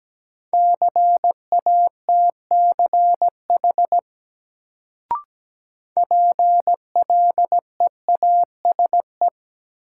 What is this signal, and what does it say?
Morse code, heard as a keyed tone, about 17 wpm, 700 Hz. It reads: CATCH PLEASE
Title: Sentences from Top 1500 Words - New Word Drill Alt - 17wpm